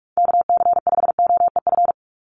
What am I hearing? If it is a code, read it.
Morse code, heard as a keyed tone, about 44 wpm, 700 Hz. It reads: KC5OEF
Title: USA Call Signs 45wpm